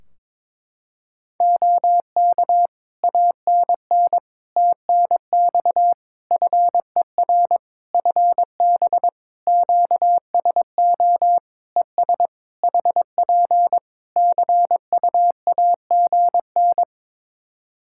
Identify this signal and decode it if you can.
Morse code, heard as a keyed tone, about 22 wpm, 700 Hz. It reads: OK ANN TNX FER FB QSO ES HP CUAGN
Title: K1USN SST QSO Practice 22wpm